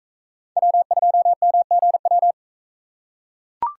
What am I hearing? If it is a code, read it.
Morse code, heard as a keyed tone, about 39 wpm, 700 Hz. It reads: W2MGW